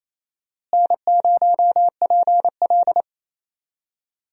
Morse code, heard as a keyed tone, about 28 wpm, 700 Hz. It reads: N0PL
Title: USA Call Signs - 4 Characters - Repeated 2x - 28wpm